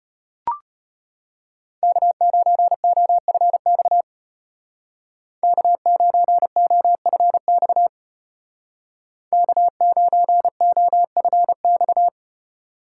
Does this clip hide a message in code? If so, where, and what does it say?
Yes: Morse code from 1.8 to 12.1 s, heard as a keyed tone, about 33 wpm, 700 Hz. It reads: K9OFX K9OFX K9OFX